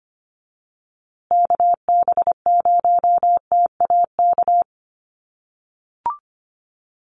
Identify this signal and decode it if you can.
Morse code, heard as a keyed tone, about 25 wpm, 700 Hz. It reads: KB0TAK
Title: USA Call Signs Repeated 3x 25wpm